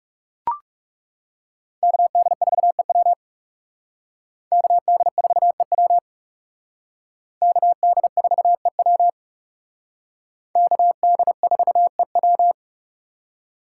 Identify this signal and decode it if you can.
Morse code, heard as a keyed tone, about 36 wpm, 700 Hz. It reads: KD4EW KD4EW KD4EW KD4EW
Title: Speed Racing - USA 2x2 Call Signs 30wpm